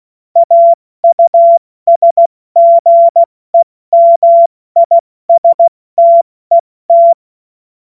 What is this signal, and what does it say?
Morse code, heard as a keyed tone, about 16 wpm, 680 Hz. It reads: AUSGEMISTET
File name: Witz-16-September-2025-70-F80